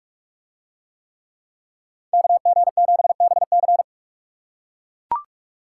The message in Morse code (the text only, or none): KG7BC